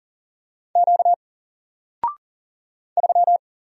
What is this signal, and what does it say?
Morse code, heard as a keyed tone, about 38 wpm, 700 Hz. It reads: Q 3
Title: Single Character (Letter:Number:Punctuation:Prosign) Alt 40wpm